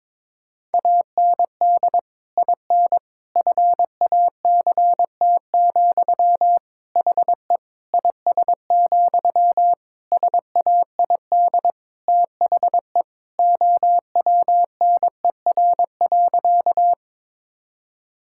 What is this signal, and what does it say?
Morse code, heard as a keyed tone, about 22 wpm, 700 Hz. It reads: AND IN FACT, HE IS, SAID THE OWNER.
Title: Sentences from Top 1600 Words 22wpm